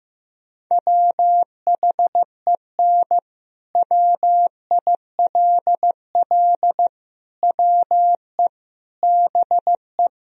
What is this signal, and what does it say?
Morse code, heard as a keyed tone, about 15 wpm, 700 Hz. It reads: WHEN WILL WE BE
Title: Sentences from Top 1000 Words - Review Part 3 - 15wpm